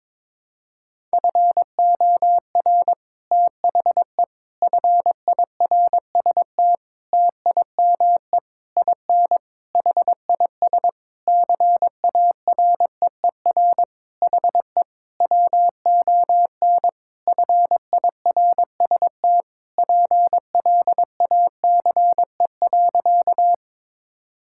Morse code, heard as a keyed tone, about 21 wpm, 700 Hz. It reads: FOR THE FIRST TIME IN HIS CAREER HE WON FIRST PLACE.
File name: Sentences with 7-Word Phrases - 22wpm